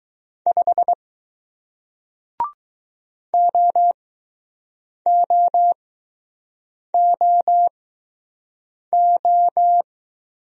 Morse code, heard as a keyed tone, about 21 wpm, 700 Hz. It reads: 5 O O O O